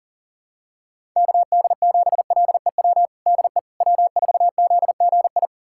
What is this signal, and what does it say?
Morse code, heard as a keyed tone, about 40 wpm, 700 Hz. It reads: KD7LEW DE W4ZGI
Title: USA Callsign DE Callsign 40wpm